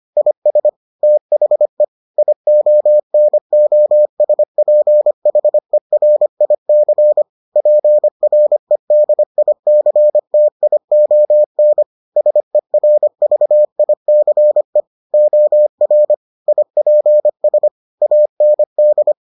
Morse code, heard as a keyed tone, about 25 wpm, 600 Hz. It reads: IS THE IONOSPHERIC PREDICTION SERVICE OR IPS AND